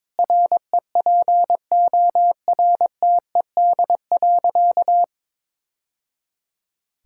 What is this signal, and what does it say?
Morse code, heard as a keyed tone, about 22 wpm, 700 Hz. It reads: REPORTED.
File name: practice / 22wpm / Sentences with 3-Word Phrases - 22wpm